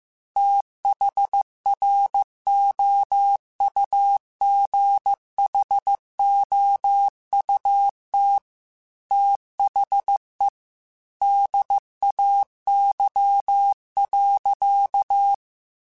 Morse code, heard as a keyed tone, about 15 wpm, 800 Hz. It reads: THROUGHOUT THE DAY.
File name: Weather Sets / Weather Sets-mp3s / Weather 5 12 @13